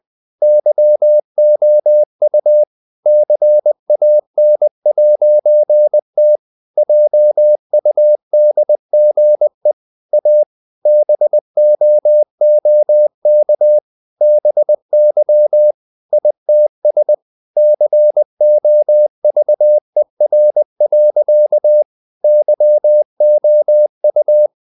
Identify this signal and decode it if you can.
Morse code, heard as a keyed tone, about 20 wpm, 600 Hz. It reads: YOU CAN'T JUDGE A BOOK BY ITS COVER. YOU
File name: CW / Practice / EnglishSayings20